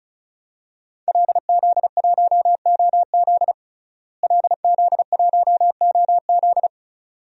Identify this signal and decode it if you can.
Morse code, heard as a keyed tone, about 34 wpm, 700 Hz. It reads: LZ1OZ LZ1OZ
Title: DXCC-100 Call Signs - 5 Characters - Repeated 2x - 35wpm